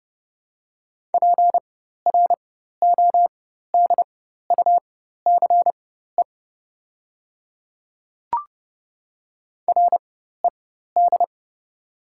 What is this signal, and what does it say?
Morse code, heard as a keyed tone, about 30 wpm, spaced as for 14 wpm, 700 Hz. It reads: PRODUCE RED
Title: 4x Character Spacing - Top 500 Words - 30wpm